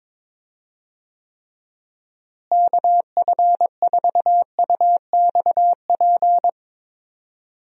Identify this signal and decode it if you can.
Morse code, heard as a keyed tone, about 22 wpm, 700 Hz. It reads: KF4UXP
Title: USA Call Signs - 6 Characters - Repeated 3x - 22wpm